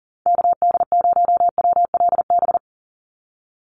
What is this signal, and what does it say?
Morse code, heard as a keyed tone, about 40 wpm, 690 Hz. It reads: KD0WLB
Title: USA Call Signs 40wpm